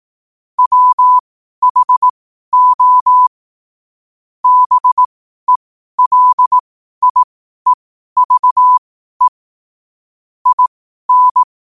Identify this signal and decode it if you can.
Morse code, heard as a keyed tone, about 18 wpm, spaced as for 12 wpm, 1000 Hz. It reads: WHO BELIEVE IN
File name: qotdfiles / 12wpm